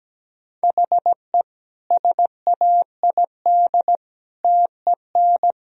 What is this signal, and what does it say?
Morse code, heard as a keyed tone, about 17 wpm, 700 Hz. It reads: HE SAID TEN
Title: Sentences from Top 1000 Words - Review Part 3 - 17wpm